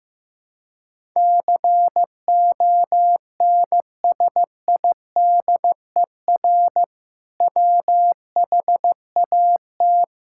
Morse code, heard as a keyed tone, about 15 wpm, 700 Hz. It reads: CONSIDER WHAT
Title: Sentences from Top 1000 Words - Review Part 5 - 15wpm